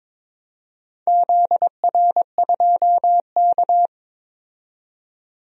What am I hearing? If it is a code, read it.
Morse code, heard as a keyed tone, about 22 wpm, 700 Hz. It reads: ZR2K
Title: DXCC-100 Call Signs - 4 Characters - 22wpm